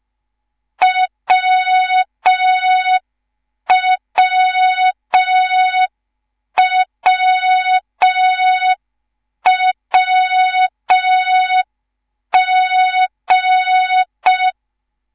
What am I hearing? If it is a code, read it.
Morse code, heard as a keyed tone, about 5 wpm, 3700 Hz. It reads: WWWWG
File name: Morze-HA2MN / hang / 10GWZ HA2MN